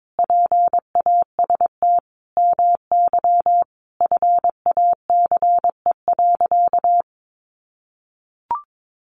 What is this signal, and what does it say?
Morse code, heard as a keyed tone, about 22 wpm, 700 Hz. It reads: PAST MY FACE.
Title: Sentences from Top 1100 Words 22wpm